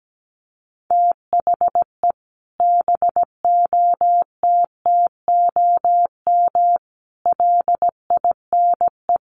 Morse code, heard as a keyed tone, about 17 wpm, 700 Hz. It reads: THE BOTTOM LINE
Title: Sets of 5 Words Encore 17wpm